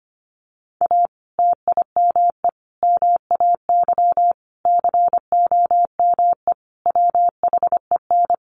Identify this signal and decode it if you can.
Morse code, heard as a keyed tone, about 25 wpm, 700 Hz. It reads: A TIME MAY COME WHEN
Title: Sentences from Top 1500 Words 25wpm